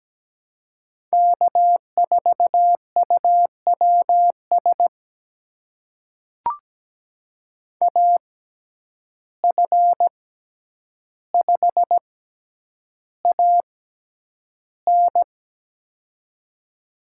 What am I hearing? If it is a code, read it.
Morse code, heard as a keyed tone, about 17 wpm, spaced as for 9 wpm, 700 Hz. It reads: K4UWS AF5AN
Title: USA Call Signs - 5 Characters - 6x Character Spacing - 17wpm